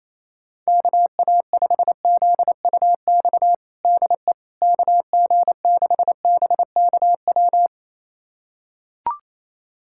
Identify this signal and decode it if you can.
Morse code, heard as a keyed tone, about 28 wpm, 700 Hz. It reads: KA5ZUX DE KG6BKW
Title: USA Callsign DE Callsign 28wpm